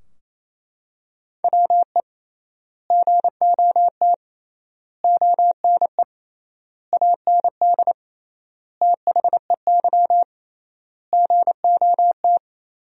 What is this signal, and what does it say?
Morse code, heard as a keyed tone, about 28 wpm, 700 Hz. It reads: WE GOT ONE AND THEY GOT
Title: 3x Word-Spacing - Sentences from Top 300 Words - 28wpm